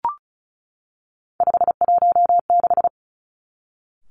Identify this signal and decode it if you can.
Morse code, heard as a keyed tone, about 36 wpm, 700 Hz. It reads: 516
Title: Three Challenging Letter-Number Combinations 35wpm